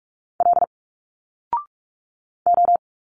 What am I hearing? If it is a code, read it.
Morse code, heard as a keyed tone, about 45 wpm, 700 Hz. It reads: L O
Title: Single Letters 45wpm